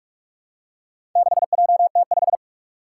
Morse code, heard as a keyed tone, about 46 wpm, 700 Hz. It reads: 6JT5
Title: Four Challenging Letter-Number Combinations 45wpm